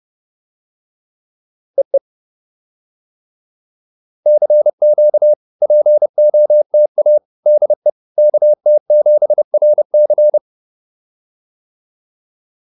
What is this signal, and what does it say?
Morse code, heard as a keyed tone, about 30 wpm, 600 Hz. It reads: EE CQ POTA DE KT7RC